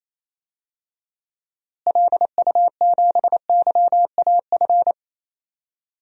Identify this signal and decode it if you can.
Morse code, heard as a keyed tone, about 28 wpm, 700 Hz. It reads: LU7YAF
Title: DXCC-100 Call Signs - 6 Characters - Repeated 3x - 28wpm